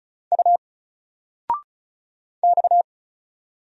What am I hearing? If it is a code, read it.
Morse code, heard as a keyed tone, about 34 wpm, 700 Hz. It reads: U X
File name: Single-Letter-Number-35wpm